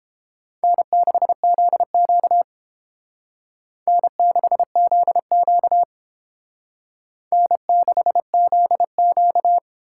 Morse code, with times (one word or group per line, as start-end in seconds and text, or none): 0.6-2.4 N6ZQ
3.9-5.8 N6ZQ
7.3-9.6 N6ZQ